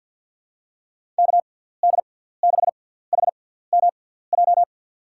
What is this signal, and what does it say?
Morse code, heard as a keyed tone, about 53 wpm, spaced as for 17 wpm, 700 Hz. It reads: KD6HMJ
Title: USA Call Signs - 6 Characters - 6x Character Spacing - 50wpm